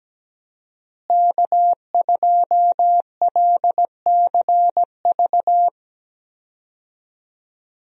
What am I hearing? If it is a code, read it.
Morse code, heard as a keyed tone, about 17 wpm, 700 Hz. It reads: K2LCV
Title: USA Call Signs - 5 Characters - Repeated 2x - 17wpm